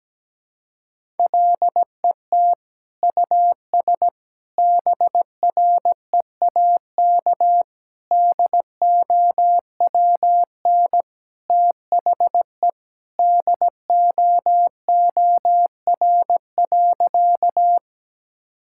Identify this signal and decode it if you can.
Morse code, heard as a keyed tone, about 17 wpm, 700 Hz. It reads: LET US BREAK DOWN THE DOOR.
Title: Sentences from Top 1000 Words - Review Part 6 - 17wpm